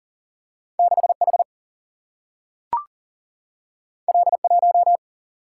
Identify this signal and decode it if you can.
Morse code, heard as a keyed tone, about 40 wpm, 700 Hz. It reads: 6 H L 1
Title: Single-Letter Single-Number 40wpm